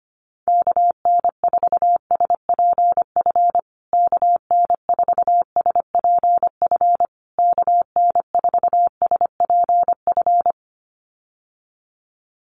Morse code, heard as a keyed tone, about 25 wpm, 700 Hz. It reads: KN4SPF KN4SPF KN4SPF
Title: USA Call Signs Repeated 3x 25wpm